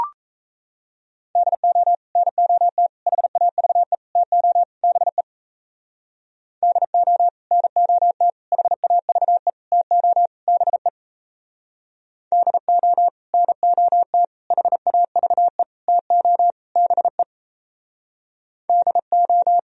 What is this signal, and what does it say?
Morse code, heard as a keyed tone, about 37 wpm, 700 Hz. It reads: DO NOT HAVE TO BE DO NOT HAVE TO BE DO NOT HAVE TO BE DO